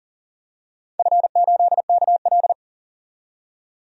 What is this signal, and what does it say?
Morse code, heard as a keyed tone, about 40 wpm, 700 Hz. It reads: F8KL
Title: DXCC-100 Call Signs - 4 Characters - 40wpm